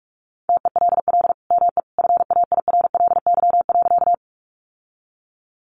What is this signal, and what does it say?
Morse code, heard as a keyed tone, about 44 wpm, 700 Hz. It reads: TELL ME FAIRLY.
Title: Sentences from Top 1100 Words 45wpm